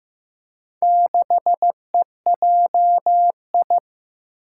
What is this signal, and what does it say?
Morse code, heard as a keyed tone, about 15 wpm, 700 Hz. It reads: 6EJI
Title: Four Challenging Letter-Number Combinations 15wpm